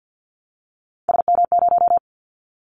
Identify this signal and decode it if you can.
Morse code, heard as a keyed tone, about 50 wpm, 700 Hz. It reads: SM0
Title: Callsign Prefixes for Top 25 DXCC - 50wpm